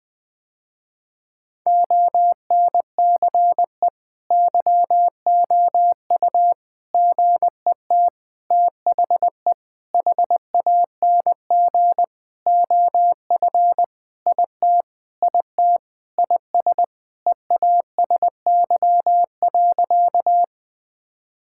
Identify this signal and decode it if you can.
Morse code, heard as a keyed tone, about 20 wpm, 700 Hz. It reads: ONCE YOU GET THE HANG OF IT IT IS EASY.
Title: Sentences with 7-Word Phrases - Spoken Beginning - 20wpm